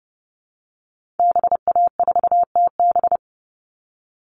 Morse code, heard as a keyed tone, about 30 wpm, 700 Hz. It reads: BA4TB
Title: Call Signs Repeated 3x - 30wpm